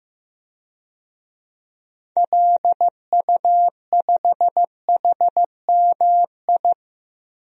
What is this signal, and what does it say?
Morse code, heard as a keyed tone, about 16 wpm, 700 Hz. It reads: LU5HMI
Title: DXCC-100 Call Signs - 6 Characters - 15wpm